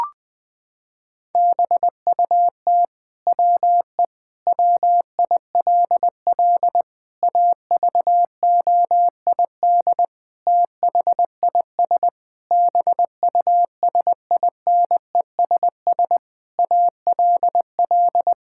Morse code, heard as a keyed tone, about 20 wpm, 700 Hz. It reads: BUT WE WILL AVOID THIS BUSINESS ALL
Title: Sentences from Top 1000 Words - Review Part 6 - 20wpm